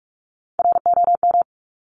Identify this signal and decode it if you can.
Morse code, heard as a keyed tone, about 46 wpm, 700 Hz. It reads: ROM
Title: Top Three Letter Combinations in Words 45wpm